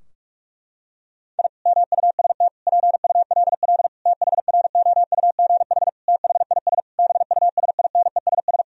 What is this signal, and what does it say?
Morse code, heard as a keyed tone, about 46 wpm, 700 Hz. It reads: I MUST PULL THROUGH THIS BUSINESS
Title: Sentences from Top 1000 Words - Review Part 2 - 45wpm